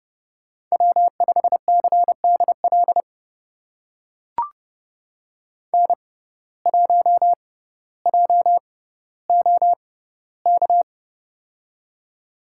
Morse code, heard as a keyed tone, about 30 wpm, spaced as for 15 wpm, 700 Hz. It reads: W5CDL N1JOK
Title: USA Call Signs - 6x Character Spacing - 30wpm